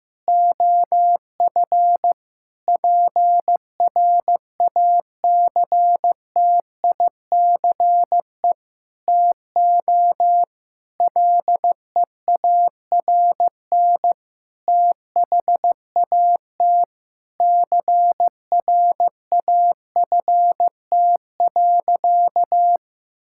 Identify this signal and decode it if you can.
Morse code, heard as a keyed tone, about 16 wpm, 700 Hz. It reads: OF PRACTICE TO LEARN THAT CRAFT.
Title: Sentences from Top 1000 Words - Review Part 6 - 15wpm